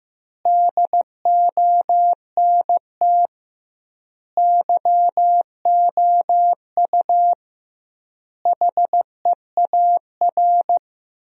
Morse code, heard as a keyed tone, about 15 wpm, 700 Hz. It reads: DONT YOU HEAR
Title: 2x Word-Spacing - Sentences from Top 300 Words - 15wpm